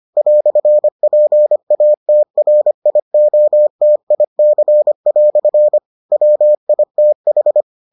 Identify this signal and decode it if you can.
Morse code, heard as a keyed tone, about 25 wpm, 600 Hz. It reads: "PATRIOTIC" WITH